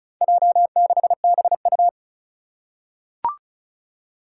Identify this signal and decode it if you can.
Morse code, heard as a keyed tone, about 36 wpm, 700 Hz. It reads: J6BU
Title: Four Challenging Letter-Number Combinations 35wpm